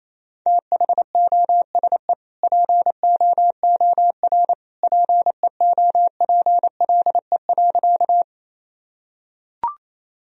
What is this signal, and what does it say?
Morse code, heard as a keyed tone, about 28 wpm, 700 Hz. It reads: THOSE POOR PEOPLE.
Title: Sentences from Top 1000 Words - Review Part 1 - 28wpm